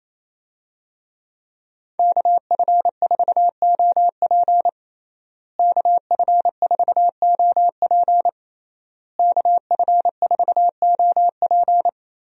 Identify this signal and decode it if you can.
Morse code, heard as a keyed tone, about 28 wpm, 700 Hz. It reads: KF4OP KF4OP KF4OP
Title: USA Call Signs - 5 Characters - Repeated 3x - 28wpm